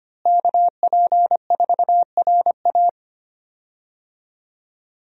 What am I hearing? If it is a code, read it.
Morse code, heard as a keyed tone, about 25 wpm, 700 Hz. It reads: KP4RA
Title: DXCC-100 Call Signs - 5 Characters - Repeated 3x - 25wpm